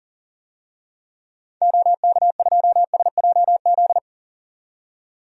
Morse code, heard as a keyed tone, about 40 wpm, 700 Hz. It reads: OK2SJZ
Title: DXCC-100 Call Signs - 6 Characters - 40wpm